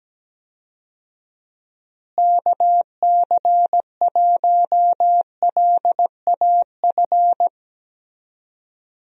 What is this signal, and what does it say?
Morse code, heard as a keyed tone, about 17 wpm, 700 Hz. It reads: KC1LAF
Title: USA Call Signs - 6 Characters - Repeated 3x - 17wpm